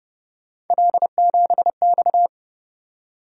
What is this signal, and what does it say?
Morse code, heard as a keyed tone, about 29 wpm, 700 Hz. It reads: L7X